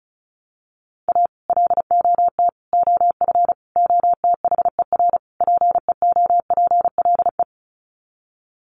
Morse code, heard as a keyed tone, about 34 wpm, 700 Hz. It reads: A LOT OF OTHER PEOPLE
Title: Sets of 5 Words Encore2 35wpm